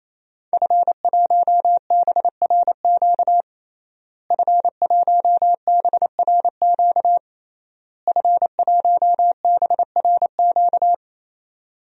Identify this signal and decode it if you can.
Morse code, heard as a keyed tone, about 28 wpm, 700 Hz. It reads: F1BRQ F1BRQ F1BRQ
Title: DXCC-100 Call Signs - 5 Characters - Repeated 3x - 28wpm